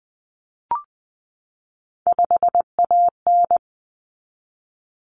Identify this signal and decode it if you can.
Morse code, heard as a keyed tone, about 20 wpm, 700 Hz. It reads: 5AN
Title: Common Cut Numbers - 3-Digit 20wpm